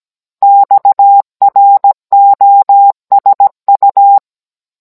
Morse code, heard as a keyed tone, about 17 wpm, 800 Hz. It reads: XROSU